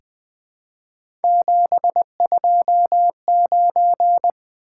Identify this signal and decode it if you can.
Morse code, heard as a keyed tone, about 20 wpm, 700 Hz. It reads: 729
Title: Three Letter-Number Combinations - 20wpm